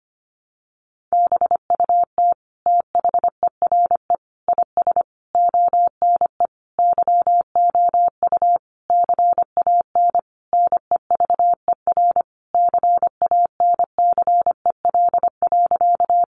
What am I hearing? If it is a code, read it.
Morse code, heard as a keyed tone, about 25 wpm, 700 Hz. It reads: BUT THERE IS ONE YOU CAN NEVER CANCEL.
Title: Sentences from Top 1200 Words 25wpm